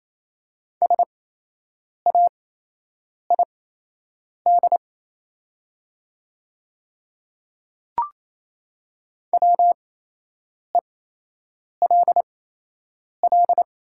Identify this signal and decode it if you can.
Morse code, heard as a keyed tone, about 27 wpm, spaced as for 8 wpm, 700 Hz. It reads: SAID WELL
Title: 8x Character Spacing - Top 200 Words - 28wpm